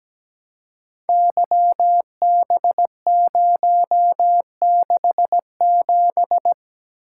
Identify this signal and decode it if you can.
Morse code, heard as a keyed tone, about 17 wpm, 700 Hz. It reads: YB067
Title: Code Groups - 5 Letter:Numbers - 17wpm